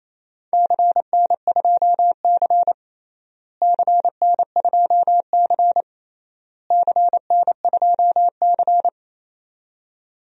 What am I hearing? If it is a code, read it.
Morse code, heard as a keyed tone, about 27 wpm, 700 Hz. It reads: CN2C CN2C CN2C
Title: DXCC-100 Call Signs - 4 Characters - Repeated 3x - 28wpm